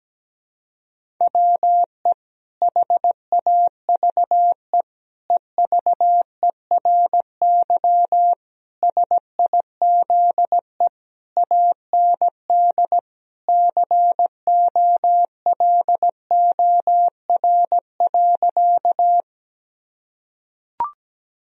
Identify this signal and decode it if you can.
Morse code, heard as a keyed tone, about 17 wpm, 700 Hz. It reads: WE HAVE EVERY SIZE AND COLOR.
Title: Sentences from Top 1000 Words - Review Part 2 - 17wpm